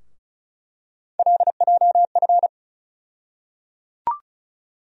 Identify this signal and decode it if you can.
Morse code, heard as a keyed tone, about 36 wpm, 700 Hz. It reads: LJF
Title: USA Call Sign Postfixes - 3 Letters 35wpm